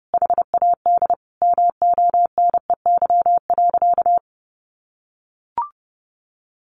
Morse code, heard as a keyed tone, about 30 wpm, 700 Hz. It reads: HAD MONEY.
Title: Sentences from Top 500 Words 30wpm